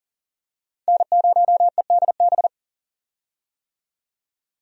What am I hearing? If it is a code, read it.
Morse code, heard as a keyed tone, about 40 wpm, 700 Hz. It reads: N0EDB
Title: USA Call Signs - 5 Characters - 40wpm